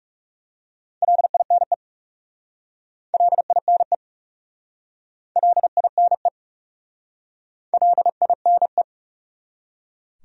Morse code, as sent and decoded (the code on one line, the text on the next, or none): .-.. .. -. / .-.. .. -. / .-.. .. -. / .-.. .. -. .
LINE LINE LINE LINE